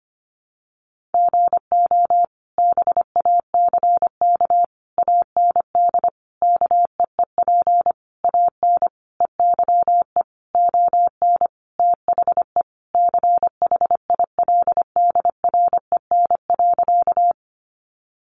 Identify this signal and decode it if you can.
Morse code, heard as a keyed tone, about 25 wpm, 700 Hz. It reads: GO BACK AND KEEP AN EYE ON THE CHILDREN.
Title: Sentences from Top 300 Words 25wpm